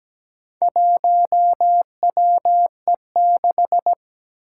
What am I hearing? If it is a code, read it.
Morse code, heard as a keyed tone, about 17 wpm, 700 Hz. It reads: 1WE6
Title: Four Challenging Letter-Number Combinations 17wpm